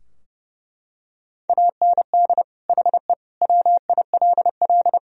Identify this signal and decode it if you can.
Morse code, heard as a keyed tone, about 29 wpm, 700 Hz. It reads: AND HE WILL
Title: Sentences from Top 1000 Words - Review Part 2 - 30wpm